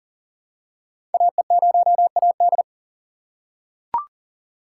Morse code, heard as a keyed tone, about 39 wpm, 690 Hz. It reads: AE0AD
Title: Code Groups - 5 Letter:Numbers - 40wpm